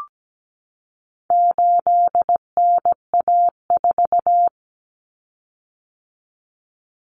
Morse code, heard as a keyed tone, about 17 wpm, 700 Hz. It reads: 8NA4
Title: Common Cut Numbers - 4-Digit 17wpm